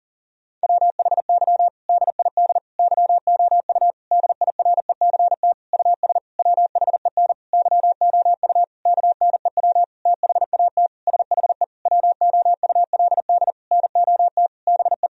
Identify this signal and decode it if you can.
Morse code, heard as a keyed tone, about 40 wpm, 700 Hz. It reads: WHY DID YOU DIRECT US WHEN YOU KNEW THAT SHE WOULD NOT BE
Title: Sentences from Top 1000 Words - Review Part 2 - 40wpm